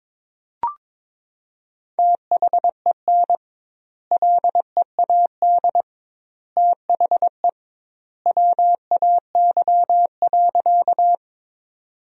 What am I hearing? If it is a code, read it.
Morse code, heard as a keyed tone, about 22 wpm, 700 Hz. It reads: THEN LEAD THE WAY.
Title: 2x Word-Spacing - Sentences from Top 500 Words - 22wpm